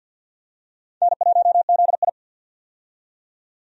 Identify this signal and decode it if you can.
Morse code, heard as a keyed tone, about 52 wpm, 700 Hz. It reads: N1ZI
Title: USA Call Signs - 4 Characters - 50wpm